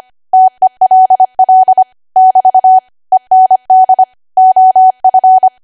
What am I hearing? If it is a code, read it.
Morse code, heard as a keyed tone, about 25 wpm, 750 Hz. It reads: TELL <BT> END OF
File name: Archive / 25 WPM / 25WPM